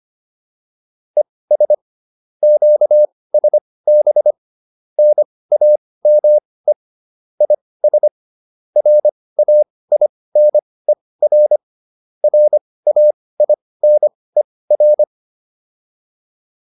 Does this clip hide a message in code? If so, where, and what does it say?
Yes: Morse code from 1.2 to 15.0 s, heard as a keyed tone, about 25 wpm, spaced as for 18 wpm, 600 Hz. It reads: ES QSB NAME IS RAINER RAINER